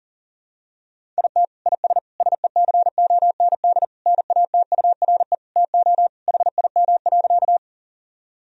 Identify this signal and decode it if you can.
Morse code, heard as a keyed tone, about 40 wpm, 700 Hz. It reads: IT IS SECOND NATURE TO HIM.
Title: Sentences from Top 1000 Words - Review Part 1 - 40wpm